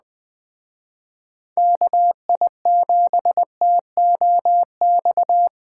Morse code, heard as a keyed tone, about 20 wpm, 700 Hz. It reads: KI7TOX
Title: Speed Racing - USA 2x3 Call Signs 17wpm